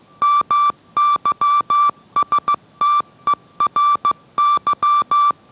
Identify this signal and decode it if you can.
Morse code, heard as a keyed tone, about 15 wpm, 1170 Hz. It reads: MYSTERY